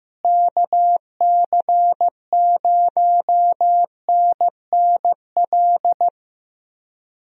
Morse code, heard as a keyed tone, about 16 wpm, 700 Hz. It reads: KC0NNL
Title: USA Call Signs - 6 Characters - Repeated 3x - 15wpm